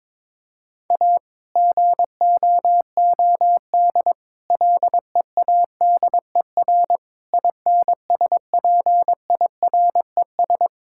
Morse code, heard as a keyed tone, about 22 wpm, 700 Hz. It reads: A GOOD LEADER INSPIRES
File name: Difficult Sentences with Hints - 22wpm